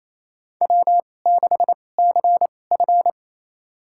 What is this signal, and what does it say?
Morse code, heard as a keyed tone, about 28 wpm, spaced as for 20 wpm, 700 Hz. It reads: W6CF
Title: USA Call Signs - 4 Characters - 2x Character Spacing - 28wpm